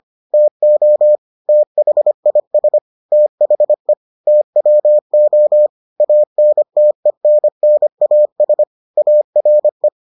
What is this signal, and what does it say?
Morse code, heard as a keyed tone, about 25 wpm, 600 Hz. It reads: TO THIS THE TWO ANTENNAS ARE